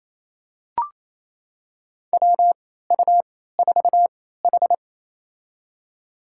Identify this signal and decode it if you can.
Morse code, heard as a keyed tone, about 27 wpm, spaced as for 16 wpm, 700 Hz. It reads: WU4H
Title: USA Call Signs - 3x Character Spacing - 28wpm